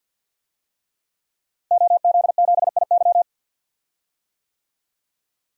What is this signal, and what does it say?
Morse code, heard as a keyed tone, about 46 wpm, 700 Hz. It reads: OZ7IY
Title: DXCC-100 Call Signs - 5 Characters - Repeated 3x - 50wpm